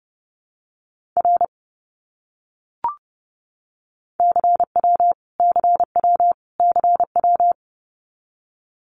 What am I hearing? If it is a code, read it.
Morse code, heard as a keyed tone, about 30 wpm, 700 Hz. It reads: R CW CW CW
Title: Abbreviations Repeated 3x - 30wpm